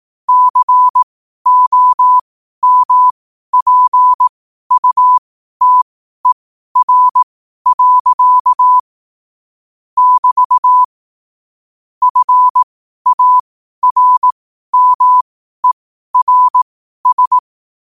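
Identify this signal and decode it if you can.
Morse code, heard as a keyed tone, about 18 wpm, spaced as for 12 wpm, 1000 Hz. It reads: COMPUTER. <BT> FARMERS
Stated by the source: Morse code quotes at 12 words per minute.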